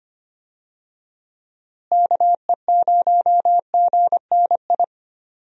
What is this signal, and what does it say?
Morse code, heard as a keyed tone, about 25 wpm, 700 Hz. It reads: KE0GNI
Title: USA Call Signs - 6 Characters - Repeated 2x - 25wpm